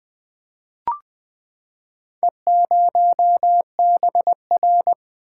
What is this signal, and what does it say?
Morse code, heard as a keyed tone, about 20 wpm, 700 Hz. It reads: E0BR
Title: Four Letter-Number Combinations - 20wpm